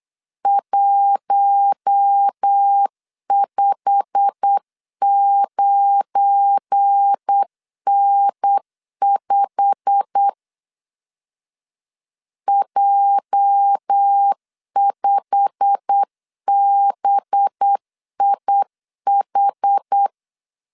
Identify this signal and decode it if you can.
Morse code, heard as a keyed tone, about 8 wpm, 800 Hz. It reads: 159N5 J5BIH
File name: LEZIONI / 8 YJ45 / 8 YJ45 8wpm 3214